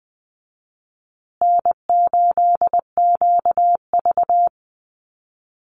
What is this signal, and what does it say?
Morse code, heard as a keyed tone, about 20 wpm, 700 Hz. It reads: N8QV
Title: USA Call Signs Repeated 2x 20wpm